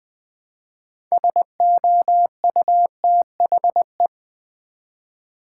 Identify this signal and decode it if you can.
Morse code, heard as a keyed tone, about 20 wpm, 700 Hz. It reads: SOUTHE
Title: Build A Word - 3 Letters At A Time - 8-Letter Words - 20wpm